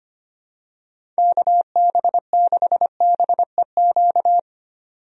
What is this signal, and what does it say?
Morse code, heard as a keyed tone, about 25 wpm, 700 Hz. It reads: KB6BEQ
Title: USA Call Signs - 6 Characters - 25wpm